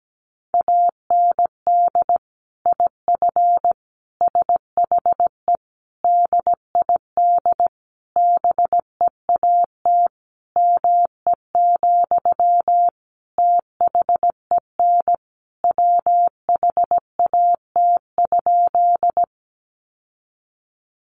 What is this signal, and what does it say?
Morse code, heard as a keyed tone, about 17 wpm, 700 Hz. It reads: AND IF SHE DID BEAT ME, THEN WHAT?
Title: Sentences from Top 1300 Words 17wpm